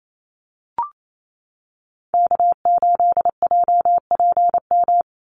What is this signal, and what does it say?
Morse code, heard as a keyed tone, about 28 wpm, 700 Hz. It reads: K8JPM
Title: Call Signs - 28wpm